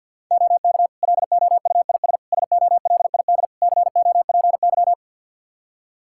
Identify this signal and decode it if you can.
Morse code, heard as a keyed tone, about 47 wpm, 700 Hz. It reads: OK LOUIS SOLID COPY
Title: Practice QSO - Exchange 3 - More Info - 50wpm